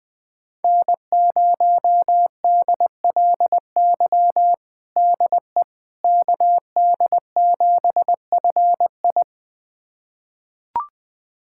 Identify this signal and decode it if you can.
Morse code, heard as a keyed tone, about 21 wpm, 700 Hz. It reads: N0DLY DE KD7FI
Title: USA Callsign DE Callsign - 5 Characters - 20wpm